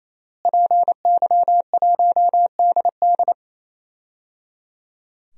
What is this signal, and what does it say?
Morse code, heard as a keyed tone, about 28 wpm, 690 Hz. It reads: PY1DD
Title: DXCC-100 Call Signs - 5 Characters - Repeated 3x - 28wpm